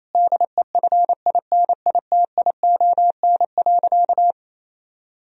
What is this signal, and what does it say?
Morse code, heard as a keyed tone, about 28 wpm, 700 Hz. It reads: DEFINITION.
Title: Sentences from Top 1000 Words - Review Part 4 - 28wpm